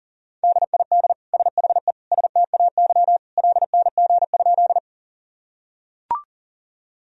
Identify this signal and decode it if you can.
Morse code, heard as a keyed tone, about 40 wpm, 700 Hz. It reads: DID SHE STAY LNG?
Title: Sentences from Top 500 Words and Abbreviations 40wpm